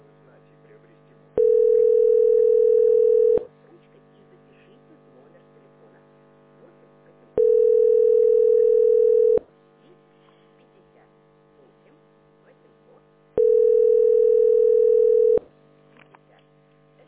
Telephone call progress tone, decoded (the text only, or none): ringback tone